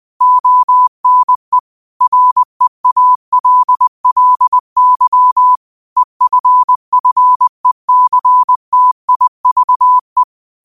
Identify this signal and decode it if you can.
Morse code, heard as a keyed tone, about 20 wpm, 1000 Hz. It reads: ONE REALLY EFFECTIVE